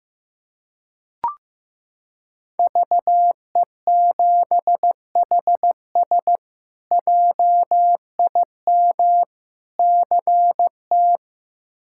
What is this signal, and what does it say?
Morse code, heard as a keyed tone, about 15 wpm, 700 Hz. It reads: VE7HS JIM CT
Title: K1USN SST QSO Practice - 15wpm